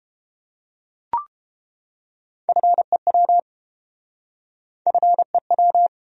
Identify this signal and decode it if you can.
Morse code, heard as a keyed tone, about 31 wpm, 700 Hz. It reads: FEW FEW